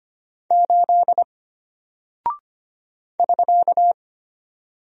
Morse code, heard as a keyed tone, about 25 wpm, 700 Hz. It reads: 8 <SK>
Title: Single Character (Letter:Number:Punctuation:Prosign) Alt 25wpm